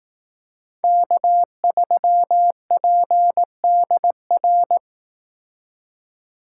Morse code, heard as a keyed tone, about 18 wpm, 700 Hz. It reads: K3PDR
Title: Speed Racing - USA 1x3 Call Signs 15wpm